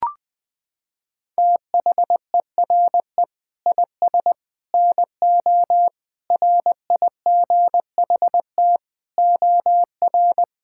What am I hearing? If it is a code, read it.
Morse code, heard as a keyed tone, about 20 wpm, 700 Hz. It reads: THERE IS NO RIGHT OR
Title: Sentences with 8-Word Phrases - 20wpm